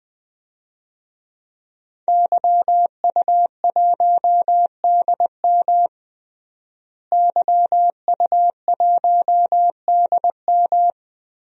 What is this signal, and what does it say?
Morse code, heard as a keyed tone, about 20 wpm, 700 Hz. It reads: YU1DM YU1DM
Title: DXCC-100 Call Signs - 5 Characters - Repeated 3x - 20wpm